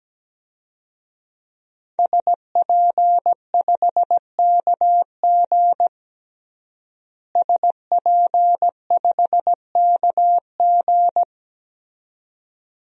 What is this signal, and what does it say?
Morse code, heard as a keyed tone, about 17 wpm, 700 Hz. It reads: SP5KG SP5KG
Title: DXCC-100 Call Signs - 5 Characters - Repeated 2x - 17wpm